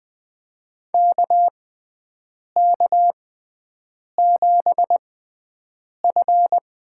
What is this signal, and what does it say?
Morse code, heard as a keyed tone, about 20 wpm, spaced as for 7 wpm, 700 Hz. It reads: KK7F
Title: USA Call Signs - 5 Characters - 6x Character Spacing - 20wpm